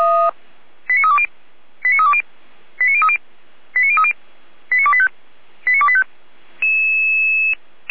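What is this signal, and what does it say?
Five-tone selective call (ZVEI), heard as quick runs of five tones, tones of about 70 ms. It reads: 89320 89320 89030 89030 89285 89285
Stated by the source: ZVEI five-tone selective call received via SDR, from the audio archive of the Feuerwehr Orlamünde fire brigade.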